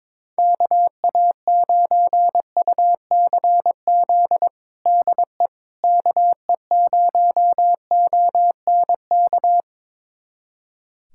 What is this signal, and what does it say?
Morse code, heard as a keyed tone, about 21 wpm, 700 Hz. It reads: KA9UCZ DE KE0ONK
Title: USA Callsign DE Callsign - 6 Characters - 22wpm